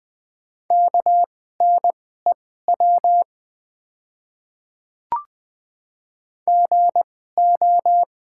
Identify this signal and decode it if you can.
Morse code, heard as a keyed tone, about 20 wpm, spaced as for 14 wpm, 700 Hz. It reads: KNEW GO
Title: 2x Character Spacing - Top 500 Words - 20wpm